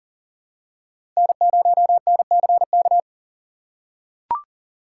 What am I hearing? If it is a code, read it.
Morse code, heard as a keyed tone, about 40 wpm, 700 Hz. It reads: N0NCK
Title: USA Call Signs - 5 Characters - 6x Character Spacing - 40wpm